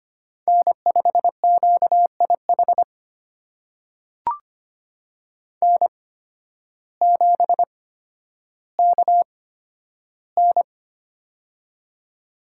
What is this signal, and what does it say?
Morse code, heard as a keyed tone, about 25 wpm, spaced as for 12 wpm, 700 Hz. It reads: N5QIH N7KN
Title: USA Call Signs - 8x Character Spacing - 25wpm